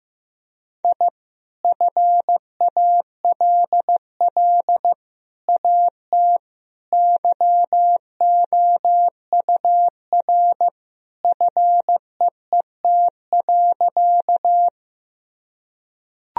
Morse code, heard as a keyed tone, about 15 wpm, 700 Hz. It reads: I FALL AT YOUR FEET.
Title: Sentences from Top 1000 Words - Review Part 2 - 15wpm